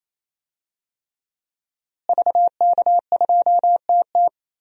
Morse code, heard as a keyed tone, about 28 wpm, 700 Hz. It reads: VK2TT
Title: DXCC-100 Call Signs - 5 Characters - Repeated 3x - 28wpm